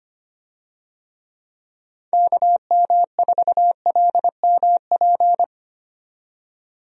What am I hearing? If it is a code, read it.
Morse code, heard as a keyed tone, about 25 wpm, 700 Hz. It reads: KM4LMP
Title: USA Call Signs - 6 Characters - 25wpm